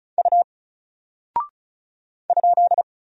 Morse code, heard as a keyed tone, about 36 wpm, 700 Hz. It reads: U ?
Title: Single Character (Letter:Number:Punctuation:Prosign) 35wpm